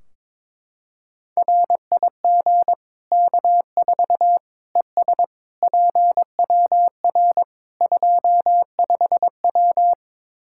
Morse code, heard as a keyed tone, about 21 wpm, 700 Hz. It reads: RIG K4 ES PWR 25W
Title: Practice QSO - Exchange 2 - Info - 22wpm